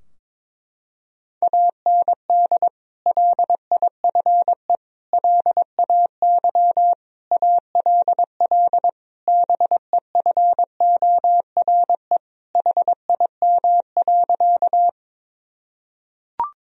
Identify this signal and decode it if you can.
Morse code, heard as a keyed tone, about 21 wpm, 700 Hz. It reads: AND LIFE LAY ALL BEFORE HIM.
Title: Sentences from Top 1000 Words - Review Part 2 - 22wpm